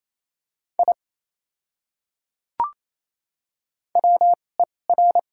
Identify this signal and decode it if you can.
Morse code, heard as a keyed tone, about 28 wpm, spaced as for 20 wpm, 700 Hz. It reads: I WER
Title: 2x Character Spacing - Top 100 Words - 28wpm